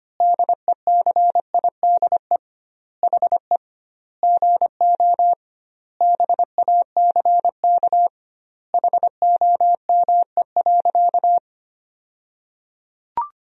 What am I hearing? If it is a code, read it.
Morse code, heard as a keyed tone, about 25 wpm, 700 Hz. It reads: DECIDE HE GO BACK HOME.
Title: 2x Word-Spacing - Sentences from Top 500 Words - 25wpm